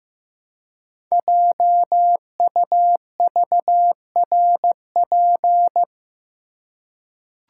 Morse code, heard as a keyed tone, about 15 wpm, 700 Hz. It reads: JUVRP
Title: Code Groups - 5 Letters - 15wpm